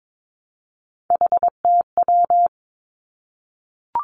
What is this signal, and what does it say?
Morse code, heard as a keyed tone, about 22 wpm, 700 Hz. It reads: HTW
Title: Three Challenging Letter-Number Combinations 22wpm